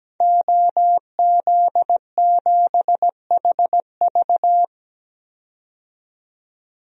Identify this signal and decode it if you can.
Morse code, heard as a keyed tone, about 17 wpm, 700 Hz. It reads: OZ7HV
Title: DXCC-100 Call Signs - 5 Characters - 17wpm